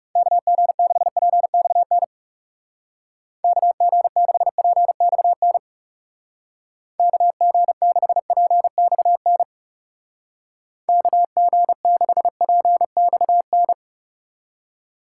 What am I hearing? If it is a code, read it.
Morse code, heard as a keyed tone, about 36 wpm, 700 Hz. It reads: KG6PXN KG6PXN KG6PXN KG6PXN
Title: Speed Racing - USA 2x3 Call Signs 30wpm